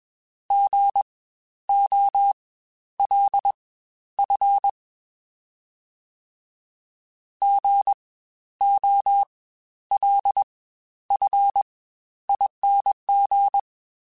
Morse code, heard as a keyed tone, about 21 wpm, spaced as for 11 wpm, 800 Hz. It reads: GOLF GOLFING
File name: Spaced Alphas